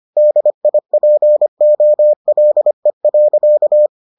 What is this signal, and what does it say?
Morse code, heard as a keyed tone, about 25 wpm, 600 Hz. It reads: DIPOLE.